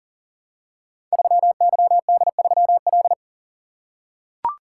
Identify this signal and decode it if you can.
Morse code, heard as a keyed tone, about 41 wpm, 700 Hz. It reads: 3YD3L
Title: Code Groups - 5 Letter:Numbers - 40wpm